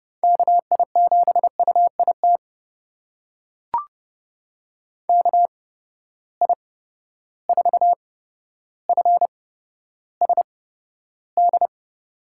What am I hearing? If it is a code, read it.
Morse code, heard as a keyed tone, about 30 wpm, spaced as for 13 wpm, 700 Hz. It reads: KI7UIT KI4FSD
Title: USA Call Signs - 6 Characters - 8x Character Spacing - 30wpm